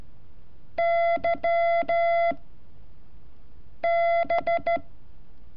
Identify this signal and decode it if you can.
Morse code, heard as a keyed tone, about 13 wpm, spaced as for 5 wpm, 680 Hz. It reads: YB
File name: curso / ejercicio 31